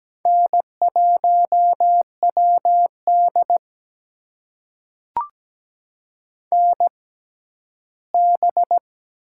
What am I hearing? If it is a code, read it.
Morse code, heard as a keyed tone, about 17 wpm, spaced as for 11 wpm, 700 Hz. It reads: N1WD NB
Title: USA Call Signs - 4 Characters - 6x Character Spacing - 17wpm